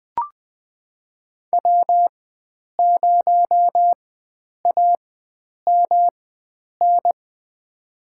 Morse code, heard as a keyed tone, about 20 wpm, spaced as for 9 wpm, 700 Hz. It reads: W0AMN
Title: USA Call Signs - 4x Character Spacing - 20wpm